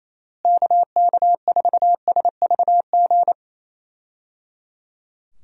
Morse code, heard as a keyed tone, about 28 wpm, 700 Hz. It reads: KK4SVG
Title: USA Call Signs - 6 Characters - Repeated 3x - 28wpm